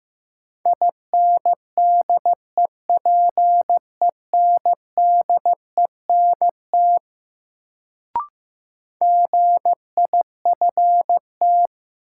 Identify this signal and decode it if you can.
Morse code, heard as a keyed tone, about 16 wpm, 700 Hz. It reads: INDEPENDENT GIFT
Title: Sentences from Top 900 Words - New Word Drill - 15wpm